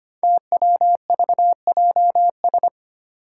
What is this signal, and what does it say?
Morse code, heard as a keyed tone, about 25 wpm, 700 Hz. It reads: TWVJS
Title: Five Challenging Letter-Number Combinations 25wpm